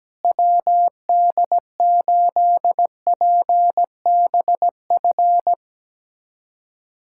Morse code, heard as a keyed tone, about 17 wpm, 700 Hz. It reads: WD8PBF
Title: USA Callsign DE Callsign - 6 Characters - 17wpm